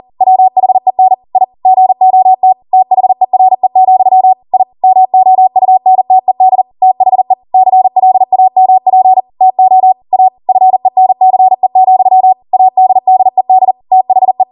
Morse code, heard as a keyed tone, about 40 wpm, 750 Hz. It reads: WHEN I GOT THERE, I MOUNTED THE CLAMP TO A FENCE, ADDED THE